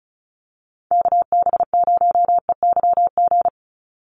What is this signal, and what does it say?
Morse code, heard as a keyed tone, about 34 wpm, 700 Hz. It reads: KB0EYG
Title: USA Call Signs 35wpm